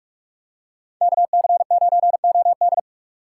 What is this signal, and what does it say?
Morse code, heard as a keyed tone, about 46 wpm, 700 Hz. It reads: KC9OD
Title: USA Call Signs - 4x Character Spacing - 45wpm